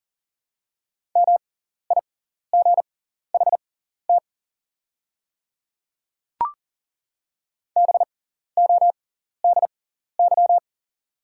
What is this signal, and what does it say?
Morse code, heard as a keyed tone, about 40 wpm, spaced as for 14 wpm, 700 Hz. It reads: MIGHT BODY
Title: 6x Character Spacing - Top 500 Words - 40wpm